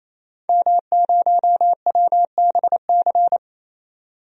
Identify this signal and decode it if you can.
Morse code, heard as a keyed tone, about 28 wpm, 700 Hz. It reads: M0WBC
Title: DXCC-100 Call Signs - 5 Characters - 28wpm